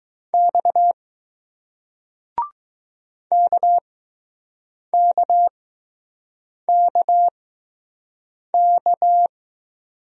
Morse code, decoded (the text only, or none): X KKKK